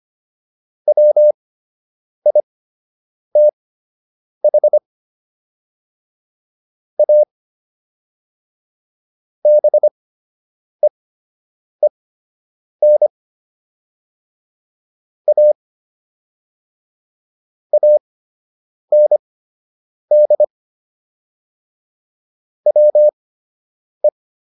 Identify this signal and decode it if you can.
Morse code, heard as a keyed tone, about 25 wpm, spaced as for 8 wpm, 600 Hz. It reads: WITH A BEEN A AND WE